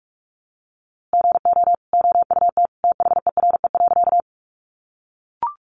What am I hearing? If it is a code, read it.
Morse code, heard as a keyed tone, about 46 wpm, 700 Hz. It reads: GO OUT THERE.
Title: A Sound Of Thunder 45wpm